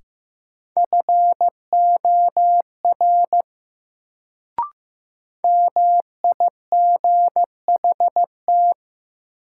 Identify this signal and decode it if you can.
Morse code, heard as a keyed tone, about 15 wpm, 700 Hz. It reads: FOR MIGHT
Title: Sets of 2 Words - New Word Drill - 15wpm